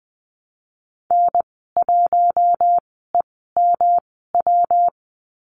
Morse code, heard as a keyed tone, about 20 wpm, spaced as for 14 wpm, 700 Hz. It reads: N1EMW
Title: 154 Lesson 25g - Letter p - Call Signs 20wpm